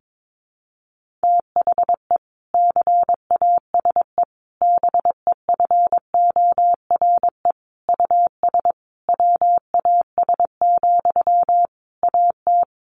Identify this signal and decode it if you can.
Morse code, heard as a keyed tone, about 22 wpm, 700 Hz. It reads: THE CASE BEFORE US WAS, AT
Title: Sentences from Top 1400 Words 22wpm